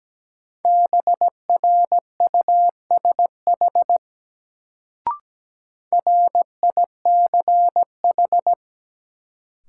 Morse code, heard as a keyed tone, about 17 wpm, 700 Hz. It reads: BRUSH RICH
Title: Sentences from Top 1100 Words - New Word Drill Alt - 17wpm